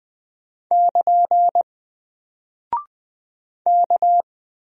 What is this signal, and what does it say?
Morse code, heard as a keyed tone, about 20 wpm, 700 Hz. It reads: <KN> K
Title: Single Character (Letter:Number:Punctuation:Prosign) Alt 20wpm